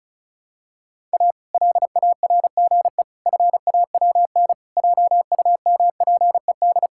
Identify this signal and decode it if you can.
Morse code, heard as a keyed tone, about 34 wpm, 700 Hz. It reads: A LARGE FAWN JUMPED
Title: Pangram Challenge v2 - 35wpm